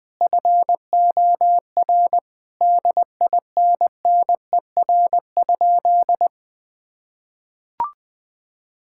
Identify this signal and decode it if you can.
Morse code, heard as a keyed tone, about 20 wpm, 700 Hz. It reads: FOR DINNER?
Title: Sentences with 4-Word Phrases - 20wpm